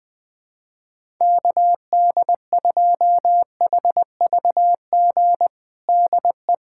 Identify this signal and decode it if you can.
Morse code, heard as a keyed tone, about 20 wpm, 700 Hz. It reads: KD2HVG DE
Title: USA Callsign DE Callsign - 6 Characters - 20wpm